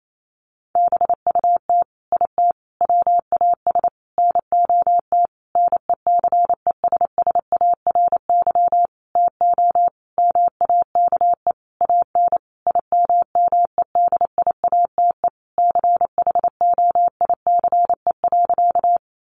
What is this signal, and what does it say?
Morse code, heard as a keyed tone, about 28 wpm, 700 Hz. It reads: BUT IT WAS NOT NECESSARY TO MAKE AN IMMEDIATE CHOICE.
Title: Sentences from Top 1200 Words 28wpm